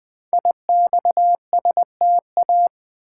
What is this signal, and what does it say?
Morse code, heard as a keyed tone, about 20 wpm, 700 Hz. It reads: IXSTA
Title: Code Groups - 5 Letters - 20wpm